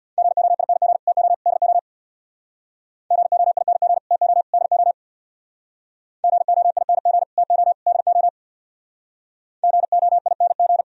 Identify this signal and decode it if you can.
Morse code, heard as a keyed tone, about 62 wpm, 700 Hz. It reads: GOING TO DO GOING TO DO GOING TO DO GOING
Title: Speed-Racing - Sets of 3 Words 50wpm